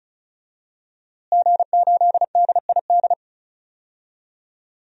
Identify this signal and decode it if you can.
Morse code, heard as a keyed tone, about 36 wpm, 700 Hz. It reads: G8DID
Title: DXCC-100 Call Signs - 5 Characters - Repeated 3x - 35wpm